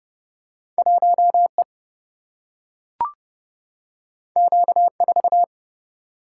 Morse code, heard as a keyed tone, about 30 wpm, 700 Hz. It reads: 1 E Q 4
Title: Single-Letter Single-Number 30wpm